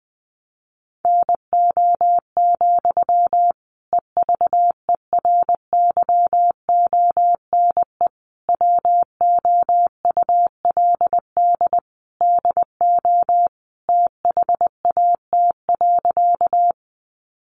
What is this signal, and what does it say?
Morse code, heard as a keyed tone, about 20 wpm, 700 Hz. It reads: NO, EVERYONE WOULD DO THAT.
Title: Sentences from Top 800 Words 20wpm